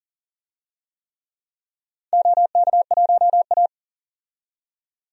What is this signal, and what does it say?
Morse code, heard as a keyed tone, about 38 wpm, 700 Hz. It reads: OK1A
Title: DXCC-100 Call Signs - 4 Characters - 40wpm